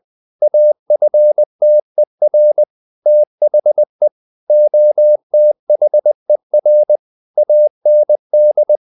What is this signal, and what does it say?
Morse code, heard as a keyed tone, about 20 wpm, 600 Hz. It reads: AFTER THE OTHER AND
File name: the old man and the sea part 2 20wpm